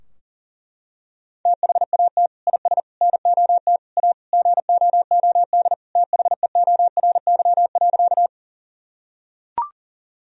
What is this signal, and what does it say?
Morse code, heard as a keyed tone, about 40 wpm, 700 Hz. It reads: THAT IS NOT A GOOD THEORY.
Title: Sentences from Top 1000 Words - Review Part 3 - 40wpm